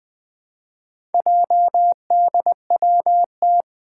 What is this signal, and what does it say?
Morse code, heard as a keyed tone, about 20 wpm, 700 Hz. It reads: JDWT
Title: Four Challenging Letter-Number Combinations 20wpm